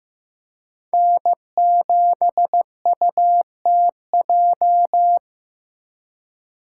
Morse code, heard as a keyed tone, about 18 wpm, 700 Hz. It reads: N7UTJ
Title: Speed Racing - USA 1x3 Call Signs 15wpm